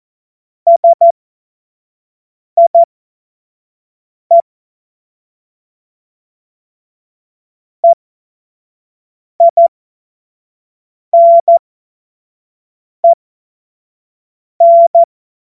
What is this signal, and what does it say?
Morse code, heard as a keyed tone, about 14 wpm, spaced as for 5 wpm, 680 Hz. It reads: SIE EINEN
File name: Morsekurs / Witze / Witz 17. August 2021, 25 F70